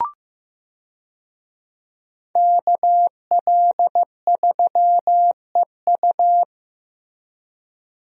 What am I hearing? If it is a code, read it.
Morse code, heard as a keyed tone, about 15 wpm, 700 Hz. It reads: KL3EU
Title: USA Call Signs - 5 Characters - Repeated 3x - 15wpm